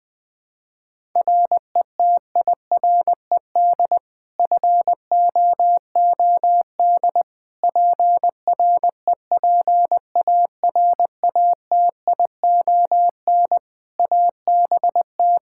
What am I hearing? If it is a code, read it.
Morse code, heard as a keyed tone, about 20 wpm, 700 Hz. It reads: RETIRED FOOD PREPARATION ABT
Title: Practice QSO - Exchange 3 - More Info - 20wpm